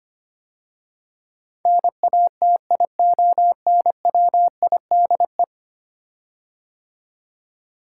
Morse code, heard as a keyed tone, about 25 wpm, 700 Hz. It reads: NATIONWIDE
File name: 10 Letter Words - First 3 Letters Given - 25wpm